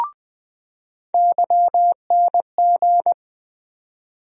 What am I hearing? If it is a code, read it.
Morse code, heard as a keyed tone, about 20 wpm, 700 Hz. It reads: YNG
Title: USA Call Sign Postfixes - 3 Letters 20wpm